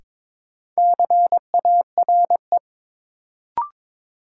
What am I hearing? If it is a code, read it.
Morse code, heard as a keyed tone, about 21 wpm, 700 Hz. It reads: CARE
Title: Sets of 3 Words - New Word Drill - 22wpm